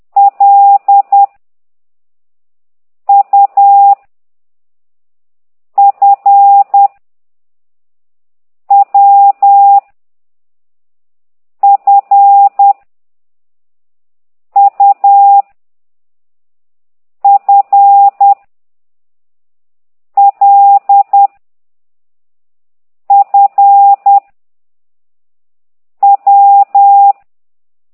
Morse code, heard as a keyed tone, about 10 wpm, 800 Hz. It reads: L U F W F U F L F W
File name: ct2idw / exerc8